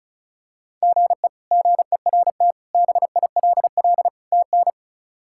Morse code, heard as a keyed tone, about 34 wpm, 700 Hz. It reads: GE GERT BILL TN
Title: K1USN SST QSO Practice - 35wpm